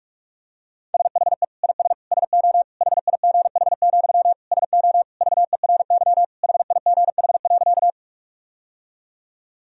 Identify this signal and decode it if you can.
Morse code, heard as a keyed tone, about 43 wpm, 700 Hz. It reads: SHE IS SO HIGH, SO VERY HIGH.
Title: Sentences from Top 1000 Words - Review Part 1 - 45wpm